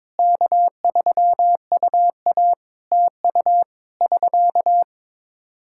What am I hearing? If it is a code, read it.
Morse code, heard as a keyed tone, about 22 wpm, 700 Hz. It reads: K3UA TU <SK>
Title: Practice QSO - Exchange 4 - Ending - 22wpm